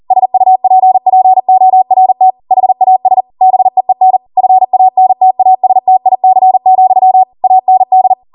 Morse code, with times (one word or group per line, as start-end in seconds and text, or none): 0.1-2.3 SUPPORT
2.5-3.2 HAS
3.4-4.2 BEEN
4.4-7.2 FANTASTIC,
7.4-8.1 AND